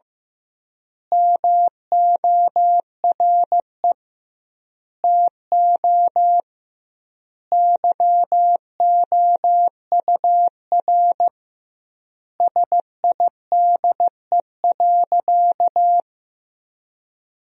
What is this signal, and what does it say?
Morse code, heard as a keyed tone, about 16 wpm, 700 Hz. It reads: MORE TO YOUR SIDE.
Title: 2x Word-Spacing - Sentences from Top 400 Words - 15wpm